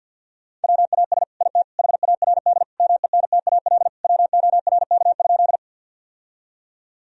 Morse code, heard as a keyed tone, about 50 wpm, 700 Hz. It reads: WAS IT HARD MENTAL WORK?
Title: Sentences from Top 1000 Words - Review Part 4 - 50wpm